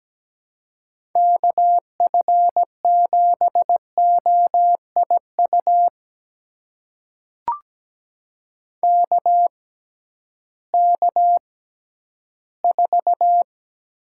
Morse code, heard as a keyed tone, about 17 wpm, spaced as for 11 wpm, 700 Hz. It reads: KF7OIU KK4
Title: USA Call Signs - 6 Characters - 6x Character Spacing - 17wpm